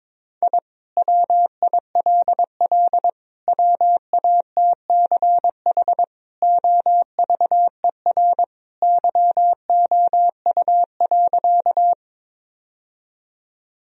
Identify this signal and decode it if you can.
Morse code, heard as a keyed tone, about 22 wpm, 700 Hz. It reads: I WILL WATCH OVER YOU.
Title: Sentences from Top 1000 Words - Review Part 1 - 22wpm